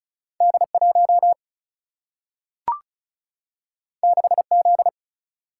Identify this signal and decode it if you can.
Morse code, heard as a keyed tone, about 36 wpm, 700 Hz. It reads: D 1 6 Z
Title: Single-Letter Single-Number 35wpm